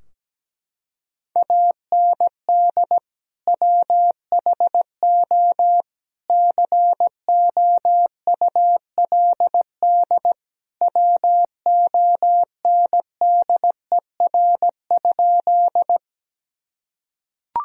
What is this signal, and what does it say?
Morse code, heard as a keyed tone, about 17 wpm, 700 Hz. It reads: AND WHO COULD WONDER?
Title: Sentences from Top 1000 Words - Review Part 3 - 17wpm